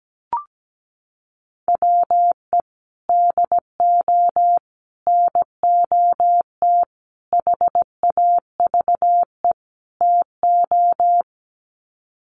Morse code, heard as a keyed tone, about 17 wpm, 700 Hz. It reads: WE DO NOT HAVE TO
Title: Sets of 5 Words 17wpm